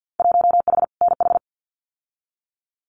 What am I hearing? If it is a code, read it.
Morse code, heard as a keyed tone, about 50 wpm, 700 Hz. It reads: 1H NH
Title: Winter Field Day Exchanges 50wpm